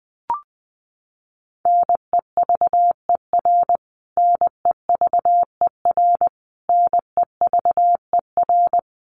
Morse code, heard as a keyed tone, about 20 wpm, 700 Hz. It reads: NEVER NEVER NEVER
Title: Top 300 Words Repeated 20wpm